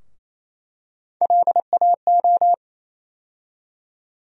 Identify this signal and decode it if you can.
Morse code, heard as a keyed tone, about 28 wpm, 700 Hz. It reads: LAO
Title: USA Call Sign Postfixes - 3 Letters 28wpm